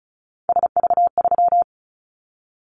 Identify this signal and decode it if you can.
Morse code, heard as a keyed tone, about 34 wpm, 700 Hz. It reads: SV3
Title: Callsign Prefixes for Top 50 DXCC - 35wpm